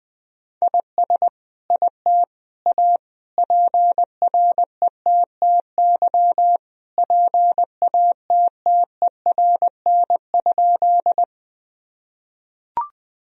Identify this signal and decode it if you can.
Morse code, heard as a keyed tone, about 20 wpm, 700 Hz. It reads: IS IT A PRETTY PATTERN?
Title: Sentences from Top 1000 Words - Review Part 2 - 20wpm